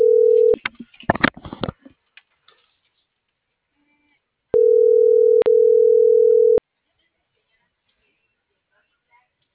Ringback tone (North America), 440 + 480 Hz.
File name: CENTRAL SKYBLUE / out lopez.brandon